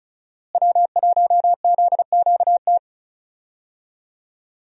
Morse code, heard as a keyed tone, about 36 wpm, 700 Hz. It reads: W1ZQT
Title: USA Call Signs - 5 Characters - Repeated 2x - 35wpm